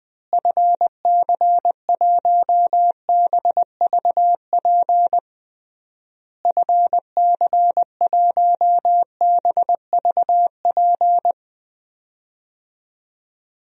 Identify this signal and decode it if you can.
Morse code, heard as a keyed tone, about 21 wpm, 700 Hz. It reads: FC1BVP FC1BVP
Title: DXCC-100 Call Signs - 6 Characters - Repeated 3x - 20wpm